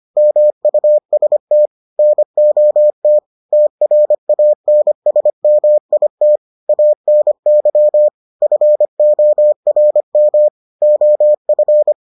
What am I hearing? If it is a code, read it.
Morse code, heard as a keyed tone, about 25 wpm, 600 Hz. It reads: MUST NOT TRANSMIT ANY FORM OF